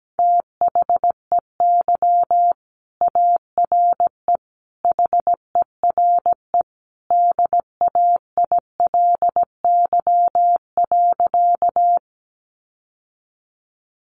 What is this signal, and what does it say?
Morse code, heard as a keyed tone, about 17 wpm, 700 Hz. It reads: THEY ARE HERE DAILY.
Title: Sentences from Top 1100 Words 17wpm